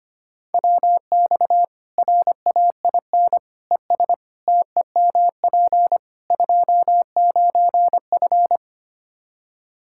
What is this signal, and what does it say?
Morse code, heard as a keyed tone, about 25 wpm, 700 Hz. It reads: WX RAIN ES TEMP 29F
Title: Practice QSO - Exchange 2 - Info - 25wpm